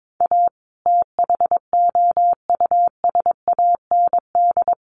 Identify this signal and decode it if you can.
Morse code, heard as a keyed tone, about 22 wpm, 700 Hz. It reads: A THOUSAND
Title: Sentences from Top 600 Words 22wpm